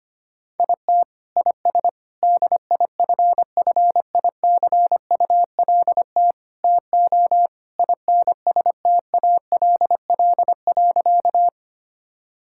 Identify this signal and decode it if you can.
Morse code, heard as a keyed tone, about 25 wpm, 700 Hz. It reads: IT IS DIFFICULT TO INSTALL.
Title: Sentences from Top 1600 Words 25wpm